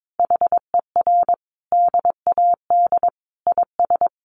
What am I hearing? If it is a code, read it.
Morse code, heard as a keyed tone, about 22 wpm, 700 Hz. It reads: HER DAD IS
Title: Additional Easy Phrases 22wpm